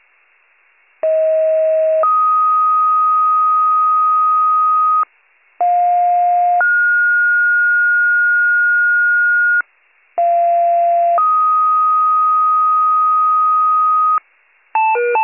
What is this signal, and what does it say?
Two-tone page, heard as a one-second tone, then a three-second tone, about 640 then 1250 Hz, about 710 then 1430 Hz, about 690 then 1250 Hz. It is heard from 1.0 to 14.2 s.